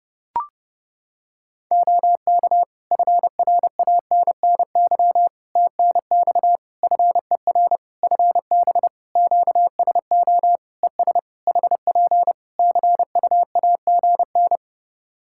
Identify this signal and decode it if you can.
Morse code, heard as a keyed tone, about 30 wpm, 700 Hz. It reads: OK FRANNY TNX FER FB QSO ES HP CUAGN
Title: Practice QSO - Exchange 4 - Ending - 30wpm